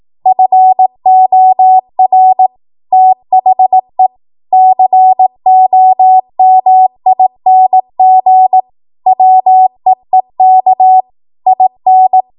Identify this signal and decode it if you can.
Morse code, heard as a keyed tone, about 18 wpm, 750 Hz. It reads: FOR THE COMING WEEK IN
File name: Archive / 18 WPM / 18WPM